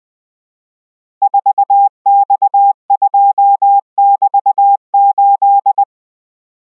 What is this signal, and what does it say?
Morse code, heard as a keyed tone, about 20 wpm, 800 Hz. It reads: 4X2<BT>8